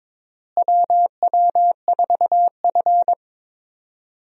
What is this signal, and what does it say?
Morse code, heard as a keyed tone, about 22 wpm, 700 Hz. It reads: WW4F
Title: USA Call Signs - 4 Characters - Repeated 3x - 22wpm